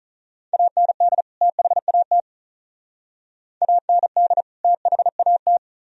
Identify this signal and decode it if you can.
Morse code, heard as a keyed tone, about 44 wpm, 700 Hz. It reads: AND THAT AND THAT